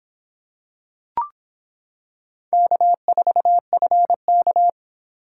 Morse code, heard as a keyed tone, about 25 wpm, 700 Hz. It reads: K4FK